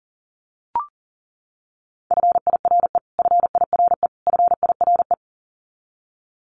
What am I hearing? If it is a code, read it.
Morse code, heard as a keyed tone, about 40 wpm, 700 Hz. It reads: FIRE FIRE FIRE
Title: Top 500 Words Repeated 40wpm